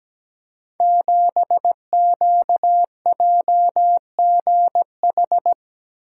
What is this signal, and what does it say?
Morse code, heard as a keyed tone, about 17 wpm, 700 Hz. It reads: 7QJGH
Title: Code Groups - 5 Letter:Numbers - 17wpm